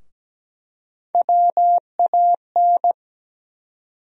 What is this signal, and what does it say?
Morse code, heard as a keyed tone, about 17 wpm, 700 Hz. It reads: WAN
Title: USA Call Sign Postfixes - 3 Letters 17wpm